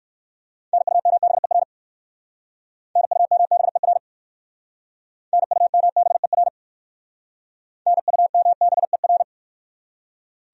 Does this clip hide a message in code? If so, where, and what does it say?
Yes: Morse code from 0.7 to 9.2 s, heard as a keyed tone, about 56 wpm, 700 Hz. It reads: NUMBER NUMBER NUMBER NUMBER